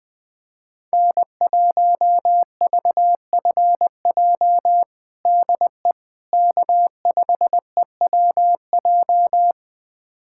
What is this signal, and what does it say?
Morse code, heard as a keyed tone, about 20 wpm, 700 Hz. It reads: N1VFJ DE K5EWJ
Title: USA Callsign DE Callsign - 5 Characters - 20wpm